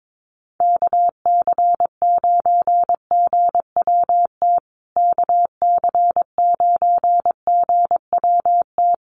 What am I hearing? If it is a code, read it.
Morse code, heard as a keyed tone, about 22 wpm, 700 Hz. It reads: KC9GWT KC9GWT
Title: USA Call Signs Repeated 3x 22wpm